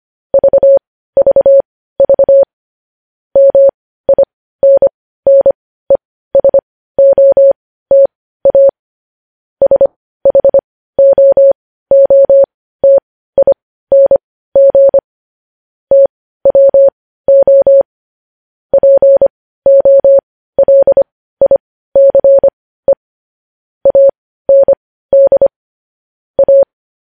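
Morse code, heard as a keyed tone, about 25 wpm, spaced as for 15 wpm, 550 Hz. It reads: VVV MINNESOTA SHOOTING TWO POLICE AND A
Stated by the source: Morse code news headlines practice at 15, preceded by vvv.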